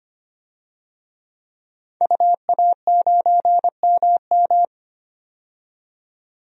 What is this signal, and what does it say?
Morse code, heard as a keyed tone, about 25 wpm, 700 Hz. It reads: UA9MM
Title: DXCC-100 Call Signs - 5 Characters - 25wpm